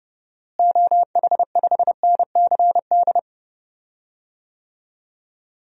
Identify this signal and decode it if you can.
Morse code, heard as a keyed tone, about 31 wpm, 700 Hz. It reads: OH5NCD
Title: DXCC-100 Call Signs - 6 Characters - Repeated 3x - 30wpm